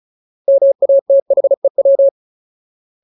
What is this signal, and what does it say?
Morse code, heard as a keyed tone, about 34 wpm, 550 Hz. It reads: MATHEW
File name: Ultimate Word List / 35WPM Ultimate